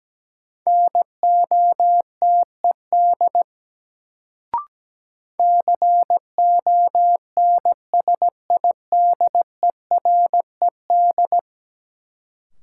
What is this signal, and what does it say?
Morse code, heard as a keyed tone, about 17 wpm, 700 Hz. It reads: NOTED CONSIDERED
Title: Sets of 5 Words - New Word Drill - 17wpm